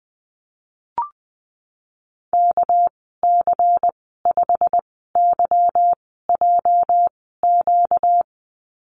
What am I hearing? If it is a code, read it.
Morse code, heard as a keyed tone, about 20 wpm, spaced as for 14 wpm, 700 Hz. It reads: KC5YJQ
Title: 245 Lesson 39g - Letter z - Call Signs 20wpm